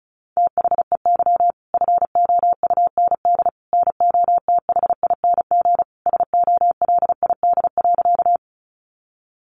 Morse code, heard as a keyed tone, about 34 wpm, 700 Hz. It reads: THEY FOUND NOTHING SOLID.
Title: Sentences from Top 1500 Words 35wpm